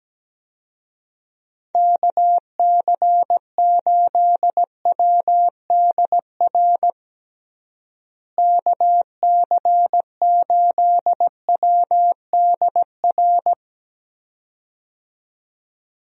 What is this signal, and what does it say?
Morse code, heard as a keyed tone, about 17 wpm, 700 Hz. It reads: KC8WDR KC8WDR
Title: USA Call Signs - 6 Characters - Repeated 2x - 17wpm